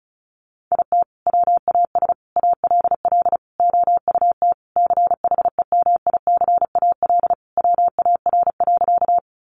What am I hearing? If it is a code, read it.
Morse code, heard as a keyed tone, about 34 wpm, 700 Hz. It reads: IT WAS ALL OUT CHEMICAL WAR.
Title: Sentences from Top 1200 Words 35wpm